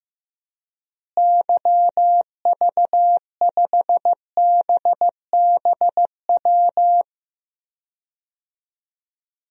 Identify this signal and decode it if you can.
Morse code, heard as a keyed tone, about 15 wpm, 700 Hz. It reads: YV5BBW
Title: DXCC-100 Call Signs - 6 Characters - Repeated 3x - 15wpm